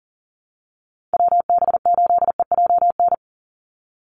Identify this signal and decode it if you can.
Morse code, heard as a keyed tone, about 40 wpm, 700 Hz. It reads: WB8EJN
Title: Call Signs Repeated 2x - 40wpm